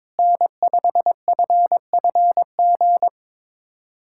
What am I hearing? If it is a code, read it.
Morse code, heard as a keyed tone, about 21 wpm, 700 Hz. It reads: N5FFG
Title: USA Callsign DE Callsign - 5 Characters - 22wpm